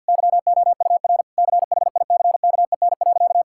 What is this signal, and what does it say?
Morse code, heard as a keyed tone, about 50 wpm, 700 Hz. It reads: YOUR CHICKEN.
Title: Sentences from Top 1000 Words - Review Part 5 - 50wpm